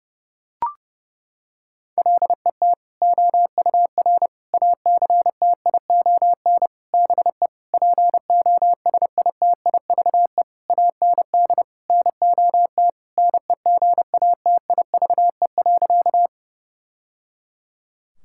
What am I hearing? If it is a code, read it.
Morse code, heard as a keyed tone, about 30 wpm, 700 Hz. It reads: LET OUR ACTION BE POSITIVE AND NOT NEGATIVE.
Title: Sentences from Top 1000 Words - Review Part 6 - 30wpm